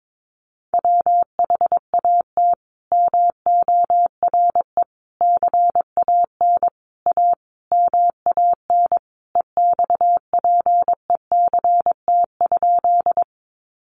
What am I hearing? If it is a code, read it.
Morse code, heard as a keyed tone, about 22 wpm, 700 Hz. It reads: WHAT MORE CAN A MAN EXPECT?
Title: Sentences from Top 1100 Words 22wpm